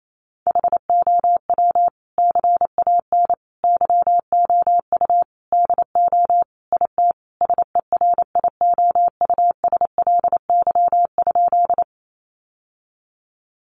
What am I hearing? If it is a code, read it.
Morse code, heard as a keyed tone, about 28 wpm, 700 Hz. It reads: HOW CAN YOU DO IT SERIOUSLY?
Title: Sentences from Top 1200 Words 28wpm